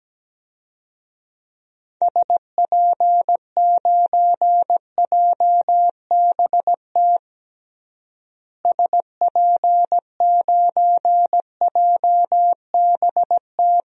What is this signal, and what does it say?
Morse code, heard as a keyed tone, about 17 wpm, 700 Hz. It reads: SP9JBT SP9JBT
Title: DXCC-100 Call Signs - 6 Characters - Repeated 3x - 17wpm